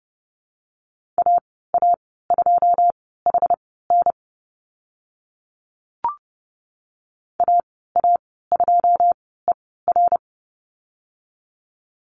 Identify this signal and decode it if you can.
Morse code, heard as a keyed tone, about 30 wpm, spaced as for 17 wpm, 700 Hz. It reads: AA2HN AA2ER
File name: Lesson 14g - Number 2 - Call Signs 30wpm